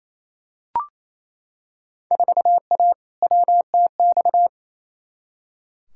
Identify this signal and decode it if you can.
Morse code, heard as a keyed tone, about 28 wpm, 700 Hz. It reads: 4A WTX
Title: Field Day Exchanges Encore - 28wpm